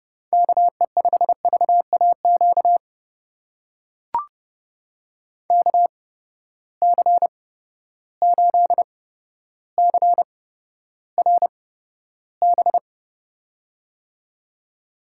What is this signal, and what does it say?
Morse code, heard as a keyed tone, about 30 wpm, spaced as for 13 wpm, 700 Hz. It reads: KE5VAQ KC8CRB
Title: USA Call Signs - 8x Character Spacing - 30wpm